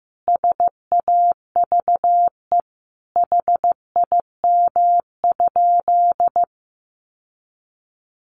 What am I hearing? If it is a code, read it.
Morse code, heard as a keyed tone, about 15 wpm, 700 Hz. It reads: SAVE HIM?
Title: Sentences from Top 1100 Words 15wpm